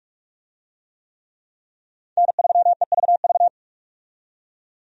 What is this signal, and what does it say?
Morse code, heard as a keyed tone, about 46 wpm, 700 Hz. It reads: N3EVV
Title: USA Call Signs - 5 Characters - 45wpm